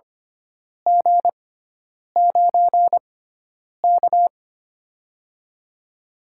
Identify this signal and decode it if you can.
Morse code, heard as a keyed tone, about 25 wpm, spaced as for 9 wpm, 700 Hz. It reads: G9K
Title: USA Call Signs - 4 Characters - 6x Character Spacing - 25wpm